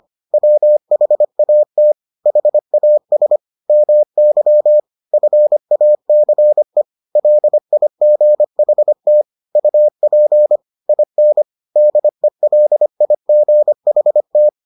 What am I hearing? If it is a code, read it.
Morse code, heard as a keyed tone, about 25 wpm, 600 Hz. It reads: WHAT HAS MY FACE LIGHT UP IN DELIGHT